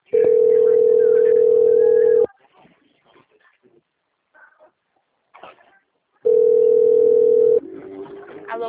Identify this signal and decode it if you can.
Ringback tone (North America), 440 + 480 Hz.
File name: out- - - - - .34470